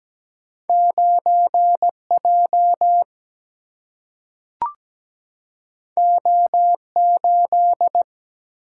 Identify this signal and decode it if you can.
Morse code, heard as a keyed tone, about 17 wpm, 700 Hz. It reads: 9 J O 8
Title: Single-Letter Single-Number 17wpm